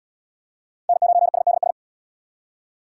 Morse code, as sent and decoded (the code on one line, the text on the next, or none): -. ----. ... -.. ...
N9SDS